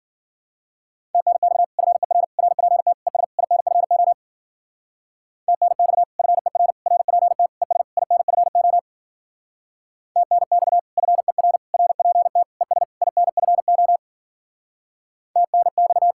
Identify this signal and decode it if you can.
Morse code, heard as a keyed tone, about 49 wpm, 700 Hz. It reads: TNX FER RPT ES INFO TNX FER RPT ES INFO TNX FER RPT ES INFO TNX
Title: Speed-Racing Common QSO Phrases - 40wpm